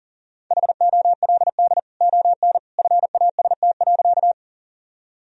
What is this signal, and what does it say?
Morse code, heard as a keyed tone, about 40 wpm, 700 Hz. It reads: HOLD ON FAST.
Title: Sentences from Top 1000 Words - Review Part 2 - 40wpm